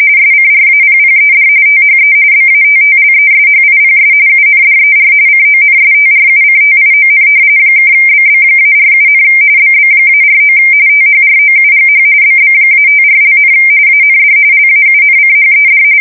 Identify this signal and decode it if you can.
RTTY, about 45.45 baud. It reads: SCRAPPED THE BATTLESHIPS UNDER CONSTRUCTION (BB-47, BB-49 THROUGH BB-54), AND LAUNCHED
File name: fsk170 / battleshipnj